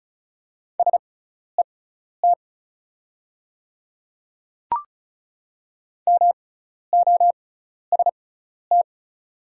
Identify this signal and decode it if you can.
Morse code, heard as a keyed tone, about 34 wpm, spaced as for 12 wpm, 700 Hz. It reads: SET MOST
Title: 6x Character Spacing - Top 300 Words - 35wpm